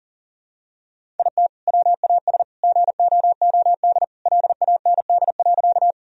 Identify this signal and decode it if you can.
Morse code, heard as a keyed tone, about 40 wpm, 700 Hz. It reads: IT WAS GOOD LAND.
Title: Sentences from Top 1000 Words - Review Part 1 - 40wpm